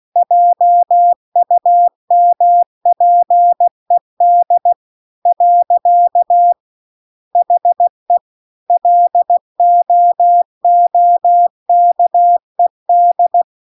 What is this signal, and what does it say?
Morse code, heard as a keyed tone, about 16 wpm, 700 Hz. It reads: JUMPED . HE LOOKED